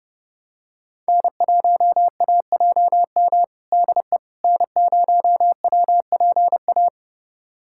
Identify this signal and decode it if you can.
Morse code, heard as a keyed tone, about 32 wpm, 700 Hz. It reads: N1AJM DE N0WPA
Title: USA Callsign DE Callsign 30wpm